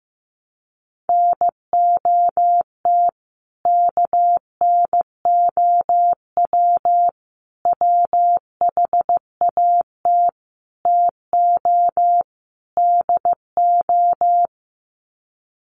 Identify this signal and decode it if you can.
Morse code, heard as a keyed tone, about 15 wpm, 700 Hz. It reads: NOT KNOW WHAT TO DO
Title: Sets of 5 Words 15wpm